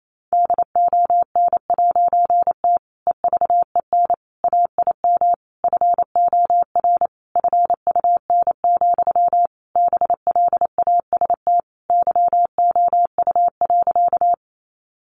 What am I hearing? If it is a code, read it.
Morse code, heard as a keyed tone, about 28 wpm, 700 Hz. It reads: DON'T EVEN AIM FOR FUN, BLAST YOU.
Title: A Sound Of Thunder 28wpm